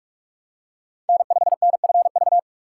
Morse code, heard as a keyed tone, about 41 wpm, 700 Hz. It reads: N5NFV
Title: USA Call Signs - 5 Characters - 6x Character Spacing - 45wpm